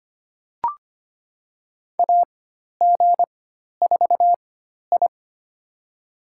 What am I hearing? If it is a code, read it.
Morse code, heard as a keyed tone, about 25 wpm, spaced as for 12 wpm, 700 Hz. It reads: AG4I
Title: USA Call Signs - 4 Characters - 4x Character Spacing - 25wpm